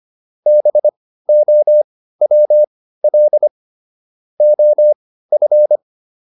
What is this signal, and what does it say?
Morse code, heard as a keyed tone, about 25 wpm, spaced as for 15 wpm, 600 Hz. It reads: BOWL OF